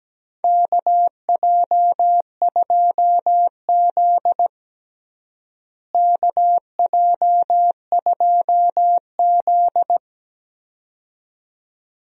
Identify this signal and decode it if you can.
Morse code, heard as a keyed tone, about 17 wpm, 700 Hz. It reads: KJ2Z KJ2Z
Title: USA Call Signs - 4 Characters - Repeated 2x - 17wpm